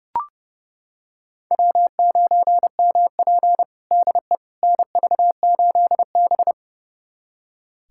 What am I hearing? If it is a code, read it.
Morse code, heard as a keyed tone, about 30 wpm, 700 Hz. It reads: W9MP DE NV8B
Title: USA Callsign DE Callsign - 4 Characters - 30wpm